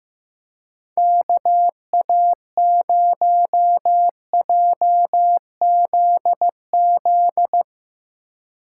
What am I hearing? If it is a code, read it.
Morse code, heard as a keyed tone, about 15 wpm, 700 Hz. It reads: KA0JZZ
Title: Speed Racing - USA Call Signs 15wpm